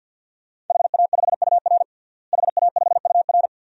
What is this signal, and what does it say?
Morse code, heard as a keyed tone, about 47 wpm, 700 Hz. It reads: HA5UR HA5UR
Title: DXCC-100 Call Signs - 5 Characters - Repeated 2x - 50wpm